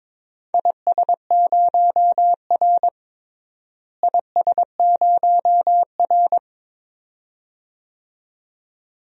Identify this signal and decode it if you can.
Morse code, heard as a keyed tone, about 22 wpm, 700 Hz. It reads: IS0R IS0R
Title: DXCC-100 Call Signs - 4 Characters - Repeated 2x - 22wpm